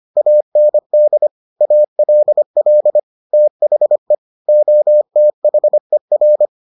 Morse code, heard as a keyed tone, about 25 wpm, 600 Hz. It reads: AND ALL THE OTHER